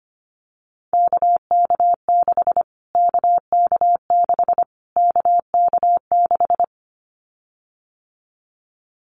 Morse code, heard as a keyed tone, about 25 wpm, 700 Hz. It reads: KK6 KK6 KK6
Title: Callsign Prefixes for Top 25 DXCC - Repeated 3x - 25wpm